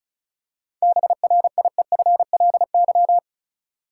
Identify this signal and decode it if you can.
Morse code, heard as a keyed tone, about 36 wpm, 700 Hz. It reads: BRIEFLY
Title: Seven Letter Words - 35wpm